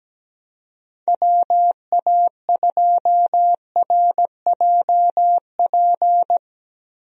Morse code, heard as a keyed tone, about 17 wpm, 700 Hz. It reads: WA2RJP
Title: USA Call Signs - 6 Characters - Repeated 2x - 17wpm